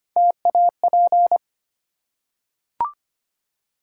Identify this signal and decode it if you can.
Morse code, heard as a keyed tone, about 25 wpm, 700 Hz. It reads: TAP
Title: Three Letter Words - 25wpm